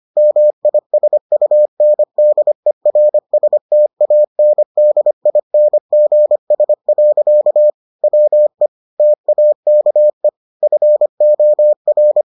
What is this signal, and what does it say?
Morse code, heard as a keyed tone, about 25 wpm, 600 Hz. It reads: MISUNDERSTANDINGS. WE TAKE FOR